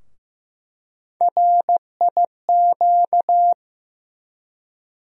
Morse code, heard as a keyed tone, about 15 wpm, 700 Hz. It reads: RIQ
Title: USA Call Sign Postfixes - 3 Letters 15wpm